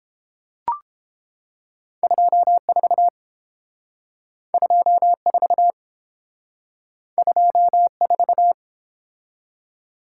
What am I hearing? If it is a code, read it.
Morse code, heard as a keyed tone, about 32 wpm, 700 Hz. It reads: 24 24 24